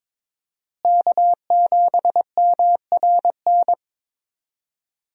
Morse code, heard as a keyed tone, about 22 wpm, 700 Hz. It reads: K7MRN
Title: USA Call Signs - 5 Characters - Repeated 3x - 22wpm